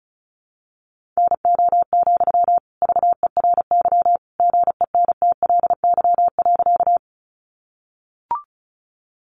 Morse code, heard as a keyed tone, about 34 wpm, 700 Hz. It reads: NO, VERY GENTLY.
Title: Sentences from Top 1500 Words 35wpm